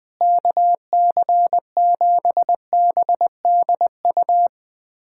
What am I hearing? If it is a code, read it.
Morse code, heard as a keyed tone, about 21 wpm, 700 Hz. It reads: KC7BDU
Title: USA Callsign DE Callsign - 6 Characters - 20wpm